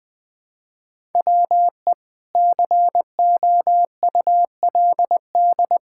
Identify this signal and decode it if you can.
Morse code, heard as a keyed tone, about 20 wpm, 700 Hz. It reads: WE COULD
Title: Pangram Challenge v2 - 20wpm